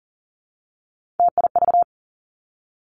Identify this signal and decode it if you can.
Morse code, heard as a keyed tone, about 40 wpm, 700 Hz. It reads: TIV
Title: Three Challenging Letter-Number Combinations 40wpm